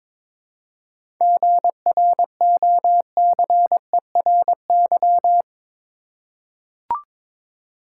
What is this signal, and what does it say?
Morse code, heard as a keyed tone, about 21 wpm, 700 Hz. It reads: GROCERY
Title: Seven Letter Words - 22wpm